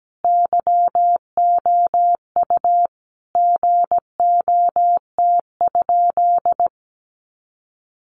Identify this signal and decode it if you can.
Morse code, heard as a keyed tone, about 17 wpm, 700 Hz. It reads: YOU GOT?
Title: Sentences from Top 300 Words 17wpm